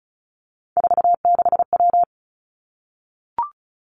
Morse code, heard as a keyed tone, about 36 wpm, 700 Hz. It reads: 46W
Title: Three Challenging Letter-Number Combinations 35wpm